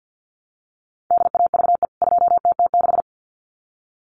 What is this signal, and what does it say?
Morse code, heard as a keyed tone, about 50 wpm, 700 Hz. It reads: DAVE 2TT6
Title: CWT Exchange 50wpm